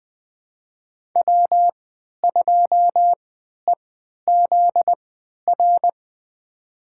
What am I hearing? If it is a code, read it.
Morse code, heard as a keyed tone, about 21 wpm, spaced as for 11 wpm, 700 Hz. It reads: W2EZR
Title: USA Call Signs - 5 Characters - 3x Character Spacing - 20wpm